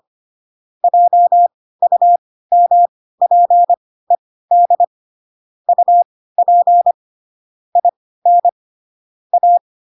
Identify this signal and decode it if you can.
Morse code, heard as a keyed tone, about 25 wpm, spaced as for 16 wpm, 700 Hz. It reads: JUMPED UP IN A
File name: Pinocchio / Pinocchio ch4 10 25 ik1hge